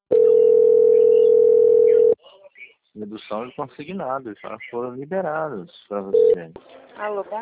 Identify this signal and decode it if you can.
Ringback tone (North America), 440 + 480 Hz, with speech.
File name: out- - - - - .28